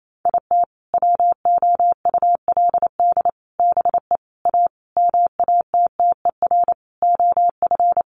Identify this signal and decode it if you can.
Morse code, heard as a keyed tone, about 28 wpm, 700 Hz. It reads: IT WOULD BE A MATTER OF
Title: Sentences from Top 1400 Words 28wpm